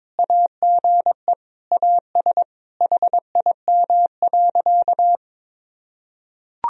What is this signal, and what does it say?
Morse code, heard as a keyed tone, about 22 wpm, 700 Hz. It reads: AGE AS HIM.
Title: Sentences from Top 1000 Words - Review Part 3 - 22wpm